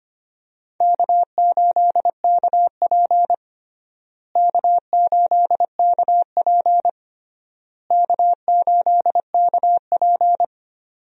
Morse code, heard as a keyed tone, about 25 wpm, 700 Hz. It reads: K8KP K8KP K8KP
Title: USA Call Signs - 4 Characters - Repeated 3x - 25wpm